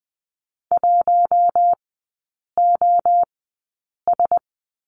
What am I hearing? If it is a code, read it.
Morse code, heard as a keyed tone, about 20 wpm, spaced as for 8 wpm, 690 Hz. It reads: 1OS
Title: 081 Lesson 14e - Number 2 - Three Characters 20wpm